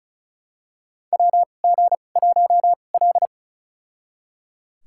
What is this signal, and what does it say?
Morse code, heard as a keyed tone, about 36 wpm, spaced as for 25 wpm, 700 Hz. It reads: WG1L
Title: USA Call Signs - 2x Character Spacing - 35wpm